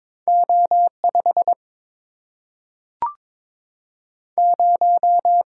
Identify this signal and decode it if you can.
Morse code, heard as a keyed tone, about 22 wpm, 700 Hz. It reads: O5 0
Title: Single-Letter Single-Number 22wpm